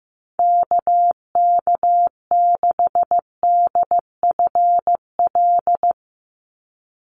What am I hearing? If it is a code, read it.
Morse code, heard as a keyed tone, about 15 wpm, 700 Hz. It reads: KK6DFL
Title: USA Call Signs 15wpm